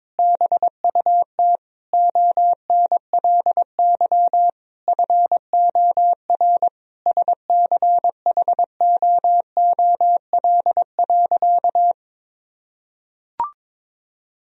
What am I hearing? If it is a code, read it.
Morse code, heard as a keyed tone, about 22 wpm, 700 Hz. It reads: BUT ONLY FOR SCHOOL.
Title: Sentences from Top 1000 Words - Review Part 1 - 22wpm